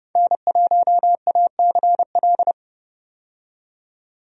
Morse code, heard as a keyed tone, about 30 wpm, 700 Hz. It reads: N1ACL
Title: USA Call Signs - 5 Characters - 30wpm